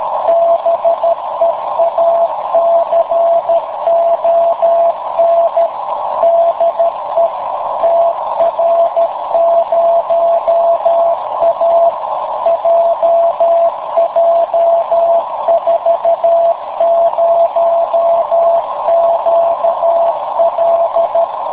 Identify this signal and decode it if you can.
Morse code, heard as a keyed tone, about 13 wpm, 670 Hz, with noise. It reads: BEACON DE TR0A JJ40QL
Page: In the afternoon at 1130z TR0A/BCN